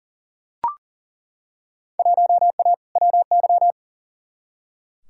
Morse code, heard as a keyed tone, about 39 wpm, 700 Hz. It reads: 1A WY
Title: Field Day Exchanges Encore - 40wpm